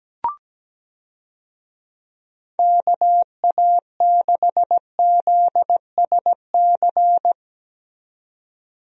Morse code, heard as a keyed tone, about 17 wpm, 700 Hz. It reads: KA6ZSC
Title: USA Call Signs - 6 Characters - 17wpm